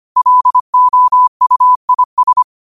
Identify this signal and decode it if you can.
Morse code, heard as a keyed tone, about 25 wpm, 1000 Hz. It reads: LOUIS